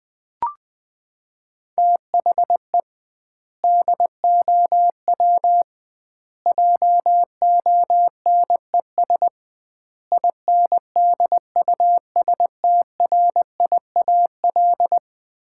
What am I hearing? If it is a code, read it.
Morse code, heard as a keyed tone, about 21 wpm, 700 Hz. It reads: THE DOW JONES INDUSTRIAL
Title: 2x Word-Spacing - Sets of 5 Words - 20wpm